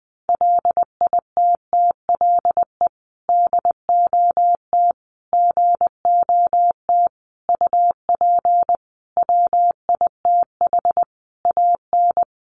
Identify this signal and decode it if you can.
Morse code, heard as a keyed tone, about 20 wpm, 700 Hz. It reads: LITTLE DOT GOT UP WITH AN
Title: Sentences from Top 1500 Words 20wpm